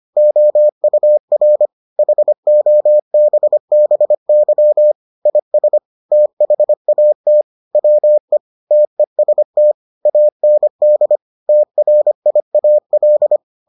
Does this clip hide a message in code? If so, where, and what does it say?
Yes: Morse code from 0.2 to 13.4 s, heard as a keyed tone, about 25 wpm, 600 Hz. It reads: OUR HOBBY IS THAT WE TEST AND TRIAL